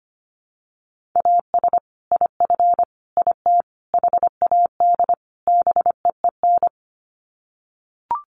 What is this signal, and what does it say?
Morse code, heard as a keyed tone, about 25 wpm, 700 Hz. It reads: AS IF IT HAD BEEN
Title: Sets of 5 Words Encore2 25wpm